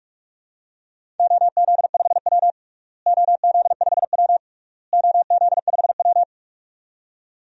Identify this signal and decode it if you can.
Morse code, heard as a keyed tone, about 46 wpm, 700 Hz. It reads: OZ5W OZ5W OZ5W
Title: DXCC-100 Call Signs - 4 Characters - Repeated 3x - 45wpm